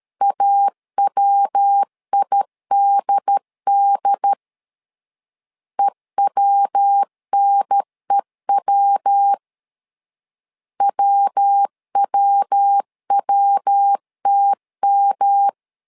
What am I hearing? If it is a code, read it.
Morse code, heard as a keyed tone, about 13 wpm, 800 Hz. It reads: AWIDD EWNEW WWWTM
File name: LEZIONI / 3 MWRO / 3 MW 12wpm 1633